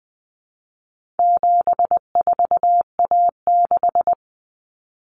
Morse code, heard as a keyed tone, about 20 wpm, 700 Hz. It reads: 74A6
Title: Common Cut Numbers - 4-Digit 20wpm